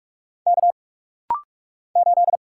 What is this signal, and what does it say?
Morse code, heard as a keyed tone, about 44 wpm, 700 Hz. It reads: K 8
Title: Single Letter-Number Alt - Rapid-Fire 45wpm